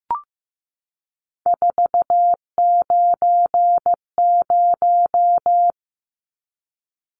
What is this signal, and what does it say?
Morse code, heard as a keyed tone, about 15 wpm, 700 Hz. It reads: 490
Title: Numbers - 3-Digit 15wpm